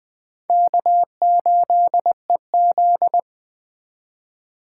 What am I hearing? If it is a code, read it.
Morse code, heard as a keyed tone, about 21 wpm, 700 Hz. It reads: K8EZ